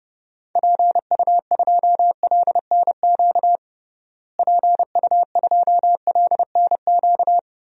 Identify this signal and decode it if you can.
Morse code, heard as a keyed tone, about 32 wpm, 700 Hz. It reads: PU2LNQ PU2LNQ
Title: DXCC-100 Call Signs - 6 Characters - Repeated 2x - 30wpm